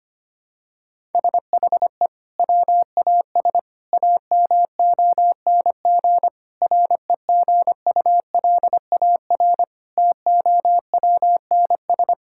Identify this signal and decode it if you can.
Morse code, heard as a keyed tone, about 25 wpm, 700 Hz. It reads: SHE WAS AMONG REGULAR TOWNS
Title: Sentences from Top 1000 Words - Review Part 6 - 25wpm